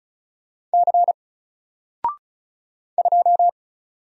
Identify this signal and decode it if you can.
Morse code, heard as a keyed tone, about 36 wpm, 700 Hz. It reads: C 2
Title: Single Character (Letter:Number:Punctuation:Prosign) 35wpm